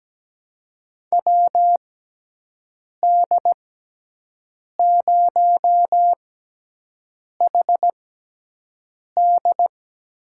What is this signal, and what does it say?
Morse code, heard as a keyed tone, about 17 wpm, spaced as for 6 wpm, 700 Hz. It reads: WD0HD
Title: USA Call Signs - 6 Characters - 6x Character Spacing - 17wpm